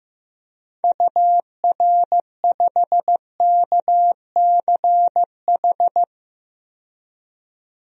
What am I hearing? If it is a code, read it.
Morse code, heard as a keyed tone, about 15 wpm, 700 Hz. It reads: UR5KCH
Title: DXCC-100 Call Signs - 6 Characters - 15wpm